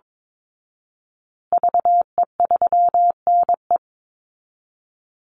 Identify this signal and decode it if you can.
Morse code, heard as a keyed tone, about 22 wpm, 700 Hz. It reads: VE3NE
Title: Call Signs Repeated 2x - 22wpm